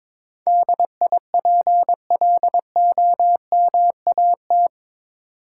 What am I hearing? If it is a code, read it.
Morse code, heard as a keyed tone, about 22 wpm, 700 Hz. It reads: DIPLOMAT
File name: practice / 22wpm / Eight Letter Words - 22wpm